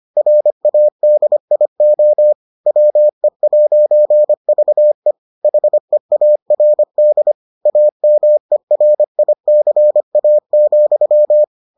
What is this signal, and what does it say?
Morse code, heard as a keyed tone, about 25 wpm, 600 Hz. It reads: RADIO WE'VE HEARD AMERICA,